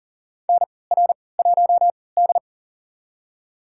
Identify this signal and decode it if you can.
Morse code, heard as a keyed tone, about 40 wpm, spaced as for 23 wpm, 700 Hz. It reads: NR1D
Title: USA Call Signs - 4 Characters - 3x Character Spacing - 40wpm